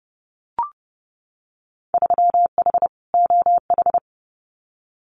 Morse code, heard as a keyed tone, about 30 wpm, 700 Hz. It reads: 3H OH
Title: Winter Field Day Exchanges Encore 30wpm